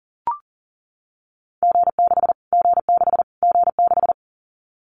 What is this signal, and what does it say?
Morse code, heard as a keyed tone, about 40 wpm, 700 Hz. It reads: G6 G6 G6
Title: Callsign Prefixes for Top 25 DXCC - Repeated 3x - 40wpm